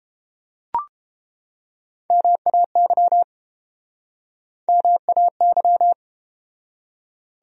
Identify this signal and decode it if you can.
Morse code, heard as a keyed tone, about 32 wpm, 700 Hz. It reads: MAY MAY